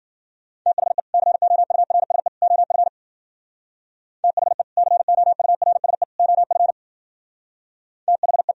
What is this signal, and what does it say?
Morse code, heard as a keyed tone, about 52 wpm, 700 Hz. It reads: THE COURSE OF THE COURSE OF THE